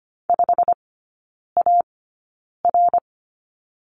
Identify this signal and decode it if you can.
Morse code, heard as a keyed tone, about 25 wpm, spaced as for 9 wpm, 690 Hz. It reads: 5AR
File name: Lesson 15e - Number 5 - Three Characters 25wpm